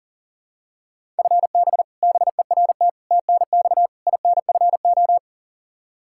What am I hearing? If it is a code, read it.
Morse code, heard as a keyed tone, about 40 wpm, 700 Hz. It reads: FB BERT TNX INFO
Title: Practice QSO - Exchange 3 - More Info - 40wpm